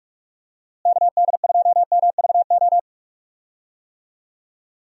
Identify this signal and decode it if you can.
Morse code, heard as a keyed tone, about 46 wpm, 700 Hz. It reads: KD2MVO
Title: USA Call Signs - 6 Characters - Repeated 2x - 45wpm